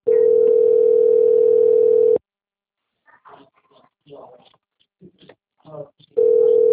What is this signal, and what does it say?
Ringback tone (North America), 440 + 480 Hz.